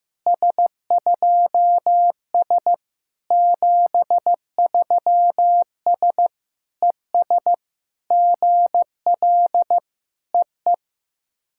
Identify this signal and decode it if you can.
Morse code, heard as a keyed tone, about 15 wpm, 690 Hz. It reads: S2S 73S ES GL EE
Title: SOTA QSO Practice 15wpm